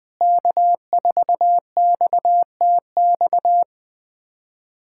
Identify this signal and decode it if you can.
Morse code, heard as a keyed tone, about 20 wpm, 700 Hz. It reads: K4XTX
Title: USA Callsign DE Callsign - 5 Characters - 20wpm